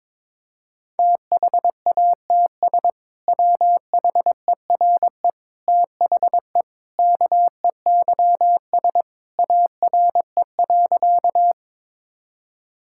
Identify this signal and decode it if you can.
Morse code, heard as a keyed tone, about 22 wpm, 700 Hz. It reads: THATS WHERE THE KEYS ARE.
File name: practice / 22wpm / Sentences with 2-Word Phrases - 22wpm